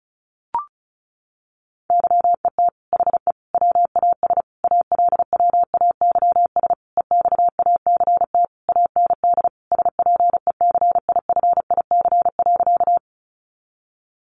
Morse code, heard as a keyed tone, about 34 wpm, 700 Hz. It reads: YET HE WAS ALWAYS EXACT AND SPECIFIC.
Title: Sentences from Top 1300 Words 35wpm